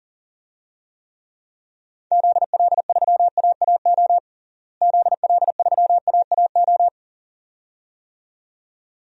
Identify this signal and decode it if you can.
Morse code, heard as a keyed tone, about 37 wpm, 700 Hz. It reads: ZL3AAO ZL3AAO
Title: DXCC-100 Call Signs - 6 Characters - Repeated 2x - 40wpm